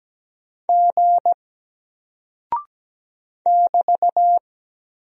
Morse code, heard as a keyed tone, about 17 wpm, 700 Hz. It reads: G <BT>
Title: Single Character (Letter:Number:Punctuation:Prosign) Alt 17wpm